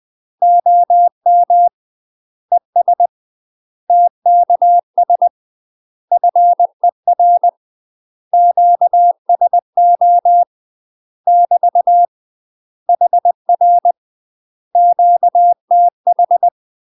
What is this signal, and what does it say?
Morse code, heard as a keyed tone, about 20 wpm, 700 Hz. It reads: OM ES TKS FER QSO <BT> HR QTH
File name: UK MASTER QSOs TRAINING / 20WPM / UK MASTER QSOs 0012